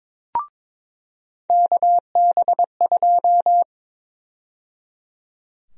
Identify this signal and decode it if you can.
Morse code, heard as a keyed tone, about 21 wpm, 700 Hz. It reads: KB2
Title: USA Call Sign Prefixes - 22wpm